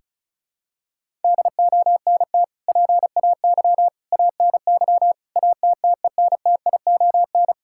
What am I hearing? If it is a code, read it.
Morse code, heard as a keyed tone, about 36 wpm, 700 Hz. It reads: DONT PAY ANY ATTENTION
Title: Sentences from Top 1000 Words - Review Part 6 - 35wpm